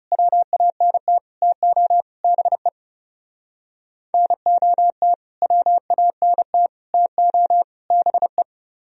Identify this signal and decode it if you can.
Morse code, heard as a keyed tone, about 36 wpm, 700 Hz. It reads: WANT TO BE NOT WANT TO BE
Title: Speed-Racing - Sets of 4 Words 30wpm